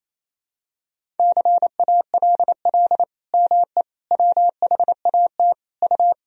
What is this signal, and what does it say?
Morse code, heard as a keyed tone, about 28 wpm, 700 Hz. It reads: CALL ME WHAT U
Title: Sentences from Top 200 Words and Abbreviations 28wpm